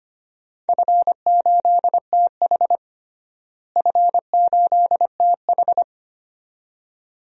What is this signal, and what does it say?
Morse code, heard as a keyed tone, about 25 wpm, 700 Hz. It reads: F8TH F8TH
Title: DXCC-100 Call Signs - 4 Characters - Repeated 2x - 25wpm